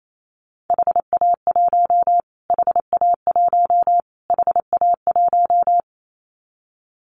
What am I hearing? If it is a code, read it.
Morse code, heard as a keyed tone, about 28 wpm, 700 Hz. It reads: HA1 HA1 HA1
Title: Callsign Prefixes for Top 25 DXCC - Repeated 3x - 28wpm